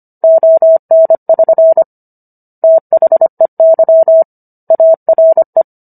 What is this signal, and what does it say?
Morse code, heard as a keyed tone, about 25 wpm, 650 Hz. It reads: ON<SN> THEY ARE